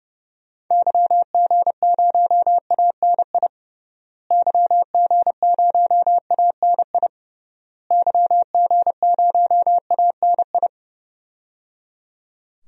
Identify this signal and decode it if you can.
Morse code, heard as a keyed tone, about 30 wpm, 700 Hz. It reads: YG0ANI YG0ANI YG0ANI
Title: DXCC-100 Call Signs - 6 Characters - Repeated 3x - 30wpm